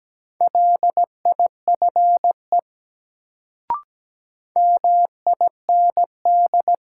Morse code, heard as a keyed tone, about 17 wpm, 700 Hz. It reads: LIFE MIND
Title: Sets of 5 Words - New Word Drill - 17wpm